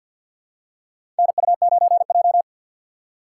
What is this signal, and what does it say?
Morse code, heard as a keyed tone, about 53 wpm, 700 Hz. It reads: NU9J
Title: USA Call Signs - 4 Characters - 6x Character Spacing - 50wpm